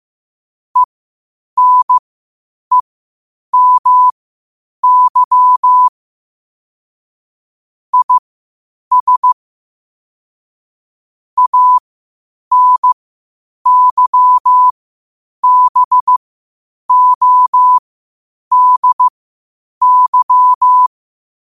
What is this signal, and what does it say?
Morse code, heard as a keyed tone, about 15 wpm, spaced as for 8 wpm, 1000 Hz. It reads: ENEMY IS ANYBODY
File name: qotdfiles / 8wpm